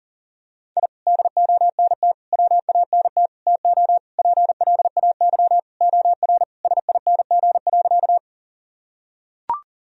Morse code, heard as a keyed tone, about 40 wpm, 700 Hz. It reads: I DONT WANT TO PLAY OR SING.
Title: Sentences from Top 1000 Words - Review Part 2 - 40wpm